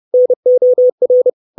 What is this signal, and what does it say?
Morse code, heard as a keyed tone, about 30 wpm, 500 Hz. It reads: NOR
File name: 632 Words 30WPM